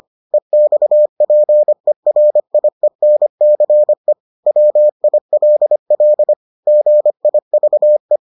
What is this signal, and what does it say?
Morse code, heard as a keyed tone, about 25 wpm, 600 Hz. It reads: EXPERIENCE WILL GIVE